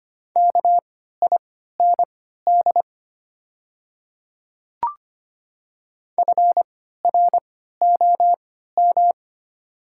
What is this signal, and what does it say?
Morse code, heard as a keyed tone, about 25 wpm, spaced as for 14 wpm, 700 Hz. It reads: KIND FROM
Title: 3x Character Spacing - Top 200 Words - 25wpm